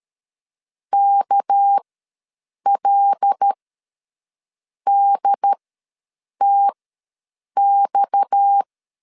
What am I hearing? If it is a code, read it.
Morse code, heard as a keyed tone, about 13 wpm, spaced as for 6 wpm, 800 Hz. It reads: KLDTX
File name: LEZIONI / 6 PKX0 / 6 PKX ws 12wpm 3106